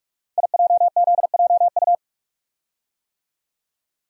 Morse code, heard as a keyed tone, about 44 wpm, 700 Hz. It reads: IJZJU
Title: Code Groups - 5 Letter:Numbers - 45wpm